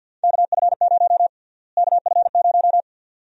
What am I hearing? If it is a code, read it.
Morse code, heard as a keyed tone, about 52 wpm, 700 Hz. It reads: KF0 KF0
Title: USA Call Sign Prefixes - Repeated 2x - 50wpm